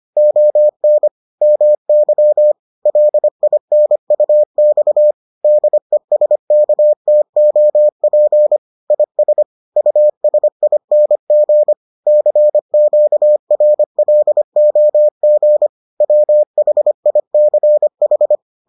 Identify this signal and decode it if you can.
Morse code, heard as a keyed tone, about 25 wpm, 600 Hz. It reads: ON MY LINUX DESKTOP IS USING CQRLOG WHICH